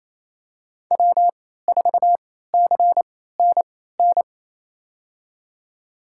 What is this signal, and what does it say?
Morse code, heard as a keyed tone, about 28 wpm, spaced as for 16 wpm, 700 Hz. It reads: W4CNN
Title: USA Call Signs - 3x Character Spacing - 28wpm